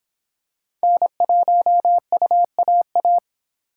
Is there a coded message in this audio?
Yes: Morse code from 0.8 to 3.2 s, heard as a keyed tone, about 26 wpm, 700 Hz. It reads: N1UAA